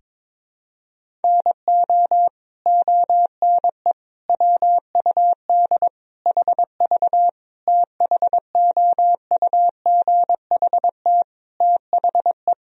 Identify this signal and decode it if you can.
Morse code, heard as a keyed tone, about 22 wpm, 700 Hz. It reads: NO ONE WUD HV THOUGHT THE
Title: Sentences from Top 300 Words and Abbreviations 22wpm